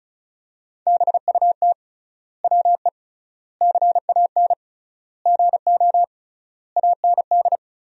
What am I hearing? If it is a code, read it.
Morse code, heard as a keyed tone, about 36 wpm, 700 Hz. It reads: BUT WE CAN GO AND
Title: 3x Word-Spacing - Sentences from Top 100 Words - 35wpm